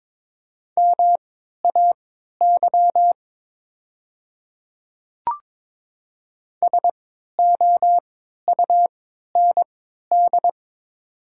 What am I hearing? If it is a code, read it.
Morse code, heard as a keyed tone, about 22 wpm, spaced as for 12 wpm, 700 Hz. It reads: MAY SOUND
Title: 3x Character Spacing - Top 100 Words - 22wpm